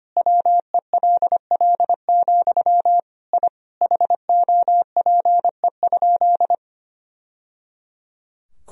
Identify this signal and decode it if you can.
Morse code, heard as a keyed tone, about 25 wpm, 700 Hz. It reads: WELL, I HOPE?
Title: Sentences from Top 1000 Words - Review Part 3 - 25wpm